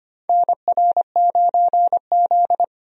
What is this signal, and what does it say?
Morse code, heard as a keyed tone, about 25 wpm, 700 Hz. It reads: NR9Z
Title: USA Call Signs - 4 Characters - 25wpm